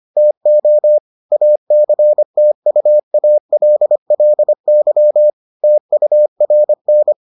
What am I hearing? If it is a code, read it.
Morse code, heard as a keyed tone, about 25 wpm, 600 Hz. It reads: TO ACTUALLY TURN